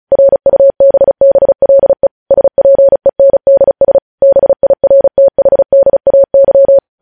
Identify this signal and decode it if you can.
Morse code, heard as a keyed tone, about 36 wpm, 550 Hz. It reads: RUBBLE SPENDS BIRTHDAY
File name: podcast / News headlines-1x-35x35